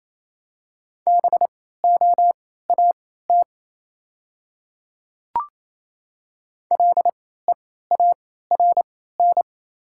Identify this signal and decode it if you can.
Morse code, heard as a keyed tone, about 28 wpm, spaced as for 16 wpm, 700 Hz. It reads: BOAT LEARN
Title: 3x Character Spacing - Top 500 Words - 28wpm